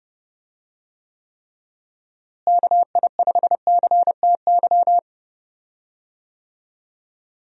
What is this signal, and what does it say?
Morse code, heard as a keyed tone, about 30 wpm, 700 Hz. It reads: KI5CTY
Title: USA Call Signs - 6 Characters - Repeated 3x - 30wpm